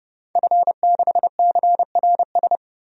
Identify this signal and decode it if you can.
Morse code, heard as a keyed tone, about 30 wpm, 700 Hz. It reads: F6CRS
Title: DXCC-100 Call Signs - 5 Characters - 30wpm